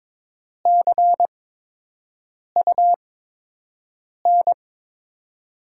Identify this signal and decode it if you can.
Morse code, heard as a keyed tone, about 22 wpm, spaced as for 6 wpm, 700 Hz. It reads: CUN
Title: USA Call Signs - 5 Characters - 8x Character Spacing - 22wpm